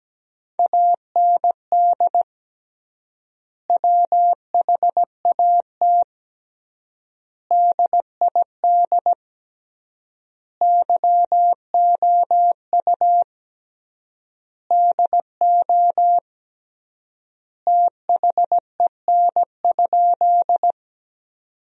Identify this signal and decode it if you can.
Morse code, heard as a keyed tone, about 17 wpm, 700 Hz. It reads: AND WHAT DID YOU DO THEN?
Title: 3x Word-Spacing - Sentences from Top 100 Words - 17wpm